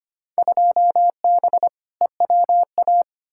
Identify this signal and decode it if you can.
Morse code, heard as a keyed tone, about 25 wpm, 700 Hz. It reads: 2B EWA
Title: Field Day Exchanges Encore - 25wpm